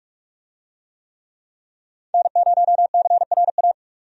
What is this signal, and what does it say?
Morse code, heard as a keyed tone, about 46 wpm, 700 Hz. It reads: N0CRA
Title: USA Call Signs - 5 Characters - 45wpm